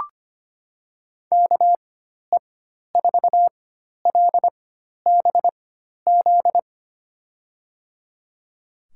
Morse code, heard as a keyed tone, about 25 wpm, spaced as for 12 wpm, 700 Hz. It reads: KE4LBZ
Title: USA Call Signs - 6 Characters - 4x Character Spacing - 25wpm